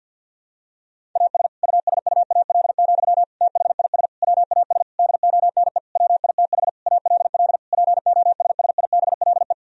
Morse code, heard as a keyed tone, about 46 wpm, 700 Hz. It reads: AS USUAL, THIS WAS DONE WITH ALL POSSIBLE
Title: Sentences from Top 1000 Words - Review Part 2 - 50wpm